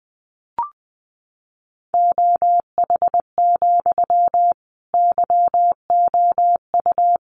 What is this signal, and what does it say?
Morse code, heard as a keyed tone, about 20 wpm, 700 Hz. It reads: OH, YOU
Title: Sentences from Top 500 Words 20wpm